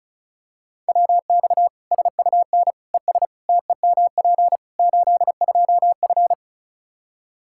Morse code, heard as a keyed tone, about 34 wpm, 700 Hz. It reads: WX SUN ES TEMP 82F
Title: K1USN SST QSO Practice 35wpm